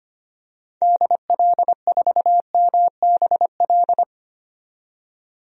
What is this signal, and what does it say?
Morse code, heard as a keyed tone, about 25 wpm, 700 Hz. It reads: DL4MBL
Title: DXCC-100 Call Signs - 6 Characters - 25wpm